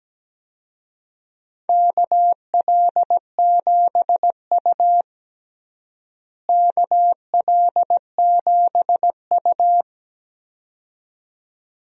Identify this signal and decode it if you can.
Morse code, heard as a keyed tone, about 17 wpm, 700 Hz. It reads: KL7U KL7U
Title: DXCC-100 Call Signs - 4 Characters - Repeated 2x - 17wpm